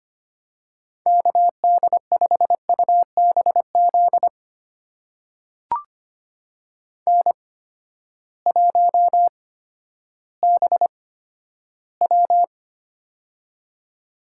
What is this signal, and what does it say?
Morse code, heard as a keyed tone, about 25 wpm, spaced as for 12 wpm, 700 Hz. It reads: KD5UBZ N1BW
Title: USA Call Signs - 8x Character Spacing - 25wpm